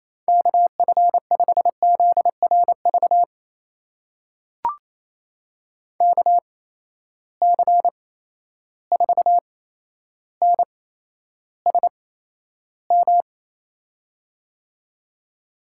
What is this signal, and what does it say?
Morse code, heard as a keyed tone, about 28 wpm, spaced as for 12 wpm, 700 Hz. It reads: KF5ZRV KC4NSM
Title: USA Call Signs - 6 Characters - 8x Character Spacing - 28wpm